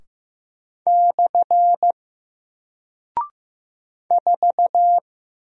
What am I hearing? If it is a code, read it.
Morse code, heard as a keyed tone, about 15 wpm, 700 Hz. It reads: / 4
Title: Single Character (Letter:Number:Punctuation:Prosign) 15wpm